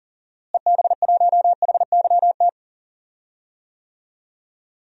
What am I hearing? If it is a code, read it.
Morse code, heard as a keyed tone, about 37 wpm, 700 Hz. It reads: EB1HYT
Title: DXCC-100 Call Signs - 6 Characters - Repeated 3x - 40wpm